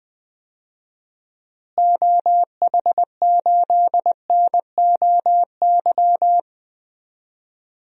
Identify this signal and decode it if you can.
Morse code, heard as a keyed tone, about 20 wpm, 700 Hz. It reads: OH8NOY
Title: DXCC-100 Call Signs - 6 Characters - Repeated 3x - 20wpm